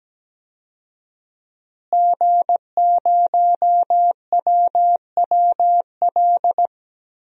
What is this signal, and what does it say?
Morse code, heard as a keyed tone, about 17 wpm, 700 Hz. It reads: G0WWL
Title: DXCC-100 Call Signs - 5 Characters - Repeated 3x - 17wpm